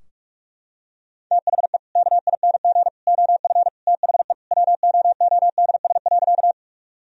Morse code, heard as a keyed tone, about 45 wpm, 700 Hz. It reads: THE KING OF THE WOODS.
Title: Sentences from Top 1000 Words - Review Part 2 - 45wpm